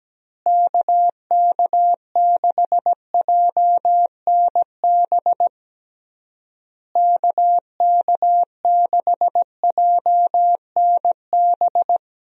Morse code, heard as a keyed tone, about 17 wpm, 700 Hz. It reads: KK6JNB KK6JNB
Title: USA Call Signs - 6 Characters - Repeated 3x - 17wpm